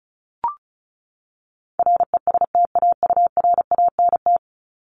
Morse code, heard as a keyed tone, about 34 wpm, 700 Hz. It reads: RESTAURANT
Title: Top 1751-2000 Words 35wpm